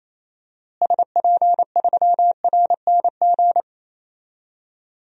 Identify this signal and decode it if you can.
Morse code, heard as a keyed tone, about 28 wpm, 700 Hz. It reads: SP3RNG
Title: DXCC-100 Call Signs - 6 Characters - Repeated 3x - 28wpm